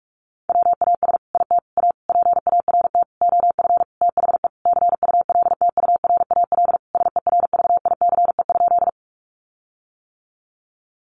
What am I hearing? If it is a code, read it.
Morse code, heard as a keyed tone, about 44 wpm, 700 Hz. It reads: WAS IT A PART OF THE CULTURAL SERVICE?
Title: Sentences from Top 1500 Words 45wpm